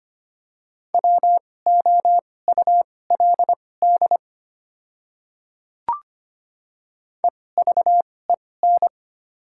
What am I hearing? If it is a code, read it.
Morse code, heard as a keyed tone, about 25 wpm, spaced as for 18 wpm, 700 Hz. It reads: WOULD EVEN
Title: 2x Character Spacing - Top 300 Words - 25wpm